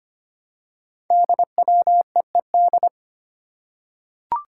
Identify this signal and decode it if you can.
Morse code, heard as a keyed tone, about 25 wpm, 700 Hz. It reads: DWEED
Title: Code Groups - 5 Letter:Numbers - 25wpm